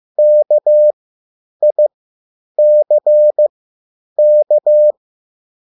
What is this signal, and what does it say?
Morse code, heard as a keyed tone, about 15 wpm, spaced as for 9 wpm, 600 Hz. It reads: KICK
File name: QSL / CW lesson